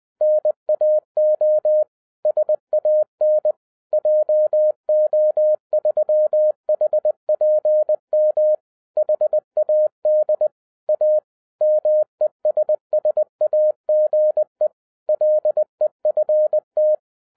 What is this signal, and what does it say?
Morse code, heard as a keyed tone, about 20 wpm, 610 Hz. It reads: NAO SAN JO3HPM HAD A MESSAGE LEFT
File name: 2012-01-mp3 / n16-THROUGH EIGHT MONTHS-20wpm